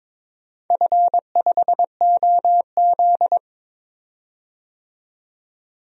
Morse code, heard as a keyed tone, about 22 wpm, 700 Hz. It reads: F5OZ
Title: DXCC-100 Call Signs - 4 Characters - 22wpm